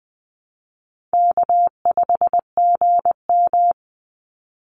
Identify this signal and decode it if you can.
Morse code, heard as a keyed tone, about 20 wpm, 700 Hz. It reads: K5GM
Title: Call Signs Repeated 3x - 20wpm